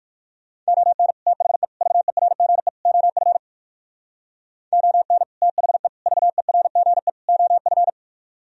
Morse code, heard as a keyed tone, about 63 wpm, 700 Hz. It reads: ON THE VERGE OF ON THE VERGE OF